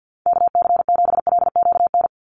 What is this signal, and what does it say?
Morse code, heard as a keyed tone, about 49 wpm, 700 Hz. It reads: KC7LQN
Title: USA Call Signs 50wpm